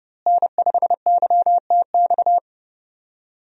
Morse code, heard as a keyed tone, about 32 wpm, 700 Hz. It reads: N5YTX
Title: USA Call Signs - 5 Characters - Repeated 3x - 30wpm